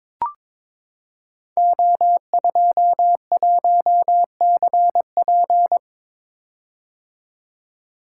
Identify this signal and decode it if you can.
Morse code, heard as a keyed tone, about 22 wpm, 700 Hz. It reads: O21CP
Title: Code Groups - 5 Letter:Numbers - 22wpm